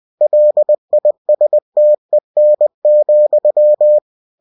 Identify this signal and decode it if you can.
Morse code, heard as a keyed tone, about 20 wpm, 600 Hz. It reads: LISTEN,
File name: cw-books / winnie-the-pooh / winnie-20wpm-page-0081